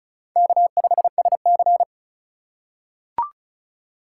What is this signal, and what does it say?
Morse code, heard as a keyed tone, about 36 wpm, 700 Hz. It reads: K5SC
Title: USA Call Signs - 3x Character Spacing - 35wpm